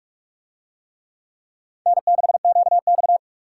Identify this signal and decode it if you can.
Morse code, heard as a keyed tone, about 43 wpm, 700 Hz. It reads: N6QX